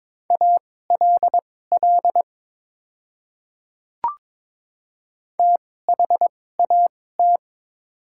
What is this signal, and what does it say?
Morse code, heard as a keyed tone, about 21 wpm, spaced as for 16 wpm, 700 Hz. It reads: ALL THAT
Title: 2x Character Spacing - Top 100 Words - 22wpm